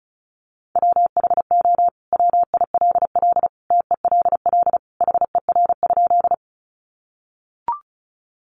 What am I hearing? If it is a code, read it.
Morse code, heard as a keyed tone, about 34 wpm, 700 Hz. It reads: WHO WILL TELL HER?
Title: Sentences from Top 200 Words 35wpm